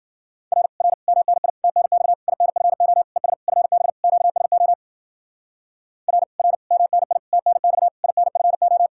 Morse code, heard as a keyed tone, about 51 wpm, 700 Hz. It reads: R R MNI TNX INFO ES FB QSO R R MNI TNX INFO